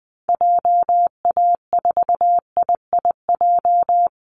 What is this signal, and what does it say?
Morse code, heard as a keyed tone, about 20 wpm, 700 Hz. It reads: JA4IIJ
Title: Call Signs - 20wpm